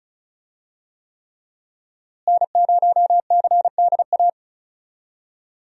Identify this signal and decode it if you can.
Morse code, heard as a keyed tone, about 36 wpm, 700 Hz. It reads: N0CDA
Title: USA Call Signs - 5 Characters - Repeated 3x - 35wpm